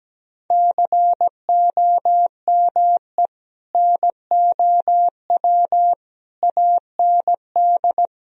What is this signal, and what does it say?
Morse code, heard as a keyed tone, about 17 wpm, 700 Hz. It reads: COME NOW AND
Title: Sentences from Top 1000 Words - Review Part 4 - 17wpm